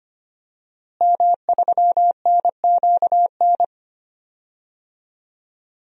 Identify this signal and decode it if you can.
Morse code, heard as a keyed tone, about 25 wpm, 700 Hz. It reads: M3NQN
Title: DXCC-100 Call Signs - 5 Characters - Repeated 2x - 25wpm